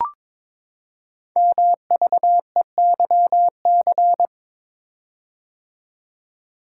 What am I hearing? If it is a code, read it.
Morse code, heard as a keyed tone, about 22 wpm, 700 Hz. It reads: MVEYC
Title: Code Groups - 5 Letters - 22wpm